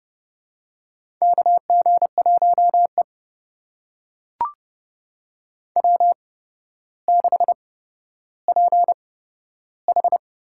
Morse code, heard as a keyed tone, about 28 wpm, spaced as for 13 wpm, 700 Hz. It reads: KG1E W6PH
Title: USA Call Signs - 4 Characters - 8x Character Spacing - 30wpm